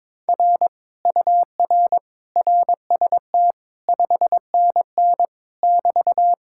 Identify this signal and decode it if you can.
Morse code, heard as a keyed tone, about 22 wpm, 700 Hz. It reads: R UR RST 5NN <BT>
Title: Speed-Racing Common QSO Phrases - 22wpm